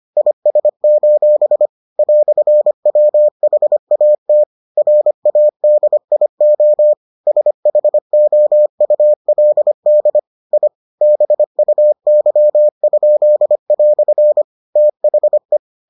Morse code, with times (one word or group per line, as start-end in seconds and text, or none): 0.2-1.7 IS:
2.0-4.4 "WHAT
4.8-6.9 RADIO
7.3-10.2 SHOULD
10.5-10.7 I
11.0-14.4 BUY?"
14.8-15.6 THE